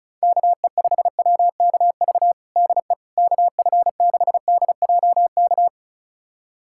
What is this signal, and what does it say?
Morse code, heard as a keyed tone, about 36 wpm, 700 Hz. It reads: KE5WKV DE KF6DJK
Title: USA Callsign DE Callsign - 6 Characters - 35wpm